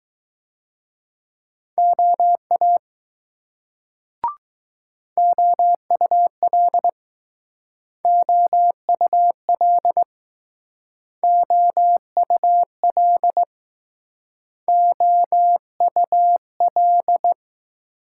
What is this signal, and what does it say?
Morse code, heard as a keyed tone, about 20 wpm, 700 Hz. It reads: OA OUL OUL OUL OUL